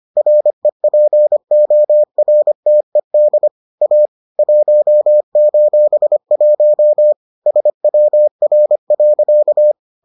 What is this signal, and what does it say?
Morse code, heard as a keyed tone, about 25 wpm, 600 Hz. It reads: REPORTED A 1:1 SWR.